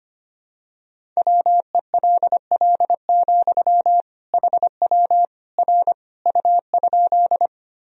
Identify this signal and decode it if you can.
Morse code, heard as a keyed tone, about 25 wpm, 700 Hz. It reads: WELL, HW R U?
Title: Sentences from Top 200 Words and Abbreviations 25wpm